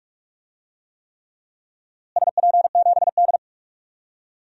Morse code, heard as a keyed tone, about 42 wpm, 700 Hz. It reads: SP7D
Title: DXCC-100 Call Signs - 4 Characters - Repeated 3x - 45wpm